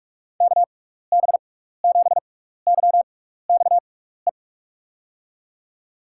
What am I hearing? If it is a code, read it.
Morse code, heard as a keyed tone, about 44 wpm, 700 Hz. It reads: K B 7 Y X E
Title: USA Call Signs - 6 Characters - 6x Character Spacing - 45wpm